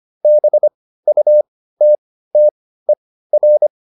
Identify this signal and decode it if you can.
Morse code, heard as a keyed tone, about 25 wpm, spaced as for 15 wpm, 600 Hz. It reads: BUTTER